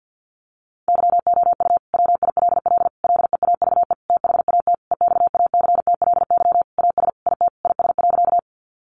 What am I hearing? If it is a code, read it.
Morse code, heard as a keyed tone, about 50 wpm, 700 Hz. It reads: YOU WILL LEAVE THAT EXACTLY AS IT IS.
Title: Sentences from Top 900 Words 50wpm